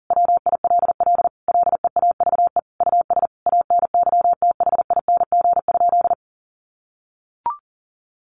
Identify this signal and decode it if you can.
Morse code, heard as a keyed tone, about 37 wpm, 700 Hz. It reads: WILL LEAVE US ANYTHING?
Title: Sentences from Top 1100 Words 40wpm